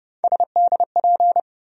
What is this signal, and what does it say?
Morse code, heard as a keyed tone, about 30 wpm, 700 Hz. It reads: SDP
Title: USA Call Sign Postfixes - 3 Letters 30wpm